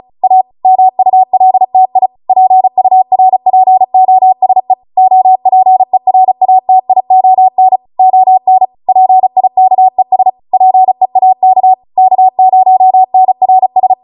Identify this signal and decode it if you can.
Morse code, heard as a keyed tone, about 36 wpm, 750 Hz. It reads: A MULTI PURPOSE OPERATION ON PIKES PEAK K0NRS